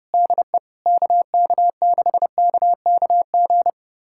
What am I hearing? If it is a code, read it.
Morse code, heard as a keyed tone, about 30 wpm, 700 Hz. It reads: DE KK6KKG
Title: USA Callsign DE Callsign - 6 Characters - 30wpm